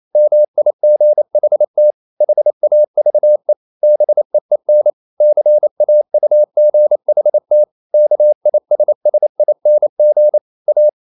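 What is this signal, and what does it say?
Morse code, heard as a keyed tone, about 28 wpm, 600 Hz. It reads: MIGHT HAVE BEEN CAUGHT KISSING A
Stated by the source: Morse code practice at 28 wpm, text of Ken Follett's novel Never in English, from CW med Gnister.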